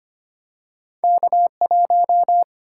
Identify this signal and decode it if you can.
Morse code, heard as a keyed tone, about 25 wpm, 690 Hz. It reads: K1
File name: Two Letter-Number Combinations - 25wpm